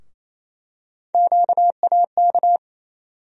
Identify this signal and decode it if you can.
Morse code, heard as a keyed tone, about 28 wpm, 700 Hz. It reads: QAK
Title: USA Call Sign Postfixes - 3 Letters 28wpm